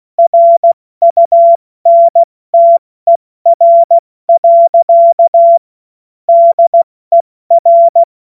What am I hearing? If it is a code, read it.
Morse code, heard as a keyed tone, about 16 wpm, 680 Hz. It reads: RUNTER. DER
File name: Morsekurs / Witze / Witz 01. November 2022 70 F80